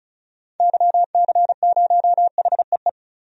Morse code, heard as a keyed tone, about 34 wpm, 700 Hz. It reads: YC0HEE
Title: DXCC-100 Call Signs - 6 Characters - Repeated 2x - 35wpm